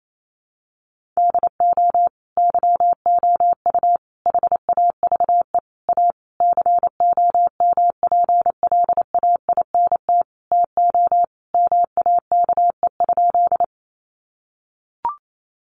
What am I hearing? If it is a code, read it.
Morse code, heard as a keyed tone, about 28 wpm, 700 Hz. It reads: DO YOU HAVE A COMPLAINT TO MAKE?
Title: Sentences from Top 1400 Words 28wpm